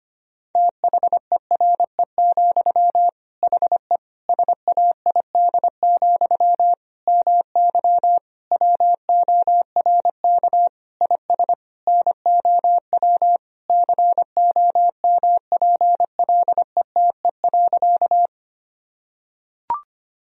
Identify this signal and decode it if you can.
Morse code, heard as a keyed tone, about 25 wpm, 700 Hz. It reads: THERE, HE SAID, MY WORK IS NOW COMPLETE.
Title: Sentences from Top 1000 Words - Review Part 2 - 25wpm